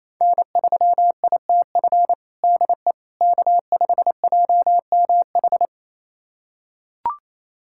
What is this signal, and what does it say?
Morse code, heard as a keyed tone, about 28 wpm, 700 Hz. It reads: N3ITF DE K5JMH
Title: USA Callsign DE Callsign - 5 Characters - 28wpm